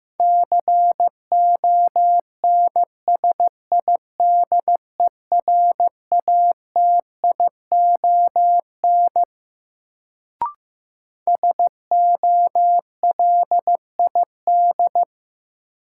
Morse code, heard as a keyed tone, about 16 wpm, 700 Hz. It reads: CONSIDERATION SOLID
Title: Sentences from Top 1500 Words - New Word Drill Alt - 15wpm